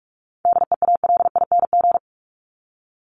Morse code, heard as a keyed tone, about 44 wpm, 700 Hz. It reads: DEALING
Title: ING Suffix - Top 100 Words with Base Words 45wpm